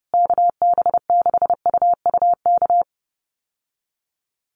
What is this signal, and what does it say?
Morse code, heard as a keyed tone, about 30 wpm, 700 Hz. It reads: KB6UUK
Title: USA Call Signs Repeated 3x 30wpm